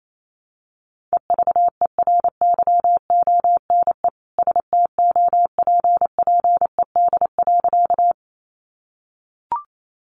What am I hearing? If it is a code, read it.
Morse code, heard as a keyed tone, about 28 wpm, 700 Hz. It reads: EVERYONE STOPPED.
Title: A Sound Of Thunder 28wpm